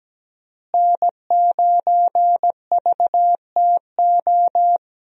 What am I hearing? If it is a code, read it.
Morse code, heard as a keyed tone, about 17 wpm, 700 Hz. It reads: N9VTO
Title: USA Call Signs - 5 Characters - Repeated 3x - 17wpm